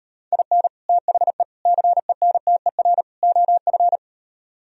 Morse code, heard as a keyed tone, about 38 wpm, 700 Hz. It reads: IN THE CENTER OF